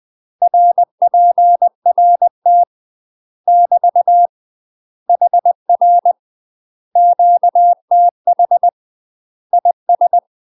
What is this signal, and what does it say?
Morse code, heard as a keyed tone, about 20 wpm, 700 Hz. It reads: RPRT <BT> HR QTH IS
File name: UK MASTER QSOs TRAINING / 20WPM / UK MASTER QSOs 0017